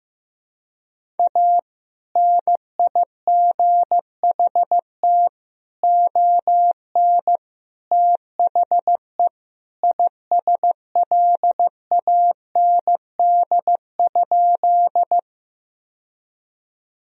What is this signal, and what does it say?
Morse code, heard as a keyed tone, about 15 wpm, 700 Hz. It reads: A NIGHT ON THE ISLAND?
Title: Sentences from Top 1000 Words - Review Part 3 - 15wpm